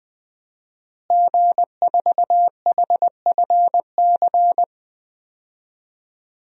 Morse code, heard as a keyed tone, about 21 wpm, 700 Hz. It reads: G4HFC
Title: DXCC-100 Call Signs - 5 Characters - Repeated 3x - 20wpm